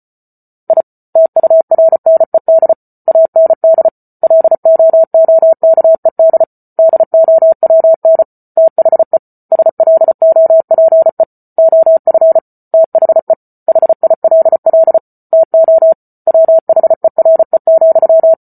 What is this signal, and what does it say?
Morse code, heard as a keyed tone, about 34 wpm, 650 Hz. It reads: I TURNED AND LOOKED DOWN THE SLOPE OF THE HILL TO WHERE,